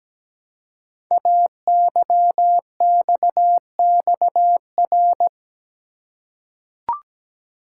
Morse code, heard as a keyed tone, about 17 wpm, 700 Hz. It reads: AYXXR
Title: Code Groups - 5 Letters - 17wpm